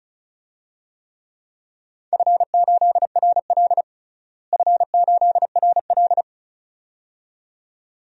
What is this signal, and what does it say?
Morse code, heard as a keyed tone, about 36 wpm, 700 Hz. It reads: F8RL F8RL
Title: DXCC-100 Call Signs - 4 Characters - Repeated 2x - 35wpm